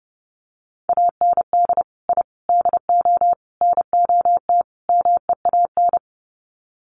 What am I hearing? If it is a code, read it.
Morse code, heard as a keyed tone, about 30 wpm, 700 Hz. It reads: AND I DO NOT MEAN
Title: Sets of 5 Words Encore2 30wpm